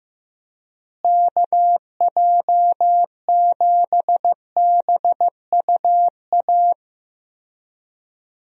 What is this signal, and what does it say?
Morse code, heard as a keyed tone, about 15 wpm, 700 Hz. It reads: KJ7BUA
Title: USA Call Signs - 6 Characters - Repeated 3x - 15wpm